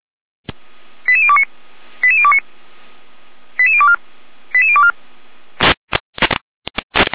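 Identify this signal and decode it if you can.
Five-tone selective call (ZVEI), heard as quick runs of five tones, tones of about 80 ms. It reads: 88029 88029 88024 88024
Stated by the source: ZVEI five-tone selective call 88801, received on an SDR.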